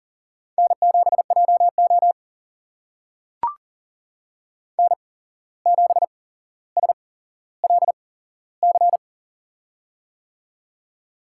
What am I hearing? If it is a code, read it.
Morse code, heard as a keyed tone, about 40 wpm, spaced as for 16 wpm, 700 Hz. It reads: N7JO N7SLC
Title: USA Call Signs - 8x Character Spacing - 40wpm